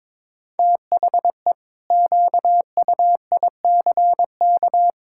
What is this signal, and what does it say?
Morse code, heard as a keyed tone, about 21 wpm, 700 Hz. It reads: THE QUICK
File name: practice / 22wpm / Sentences with 8-Word Phrases - 22wpm